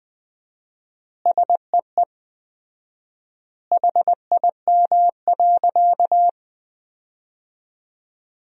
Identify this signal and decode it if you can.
Morse code, heard as a keyed tone, about 21 wpm, 700 Hz. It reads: SEE HIM.
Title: 4x Word-Spacing - Sentences from Top 300 Words - 20wpm